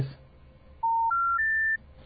Special information tone.